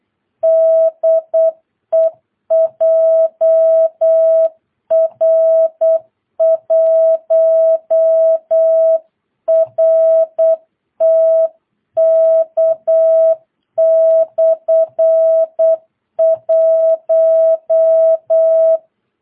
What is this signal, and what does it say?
Morse code, heard as a keyed tone, about 8 wpm, 650 Hz. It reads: DE JR1RTK/1
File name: jh8cbh / 2022cw